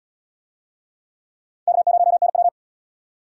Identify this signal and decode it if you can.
Morse code, heard as a keyed tone, about 75 wpm, 700 Hz. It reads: K0NW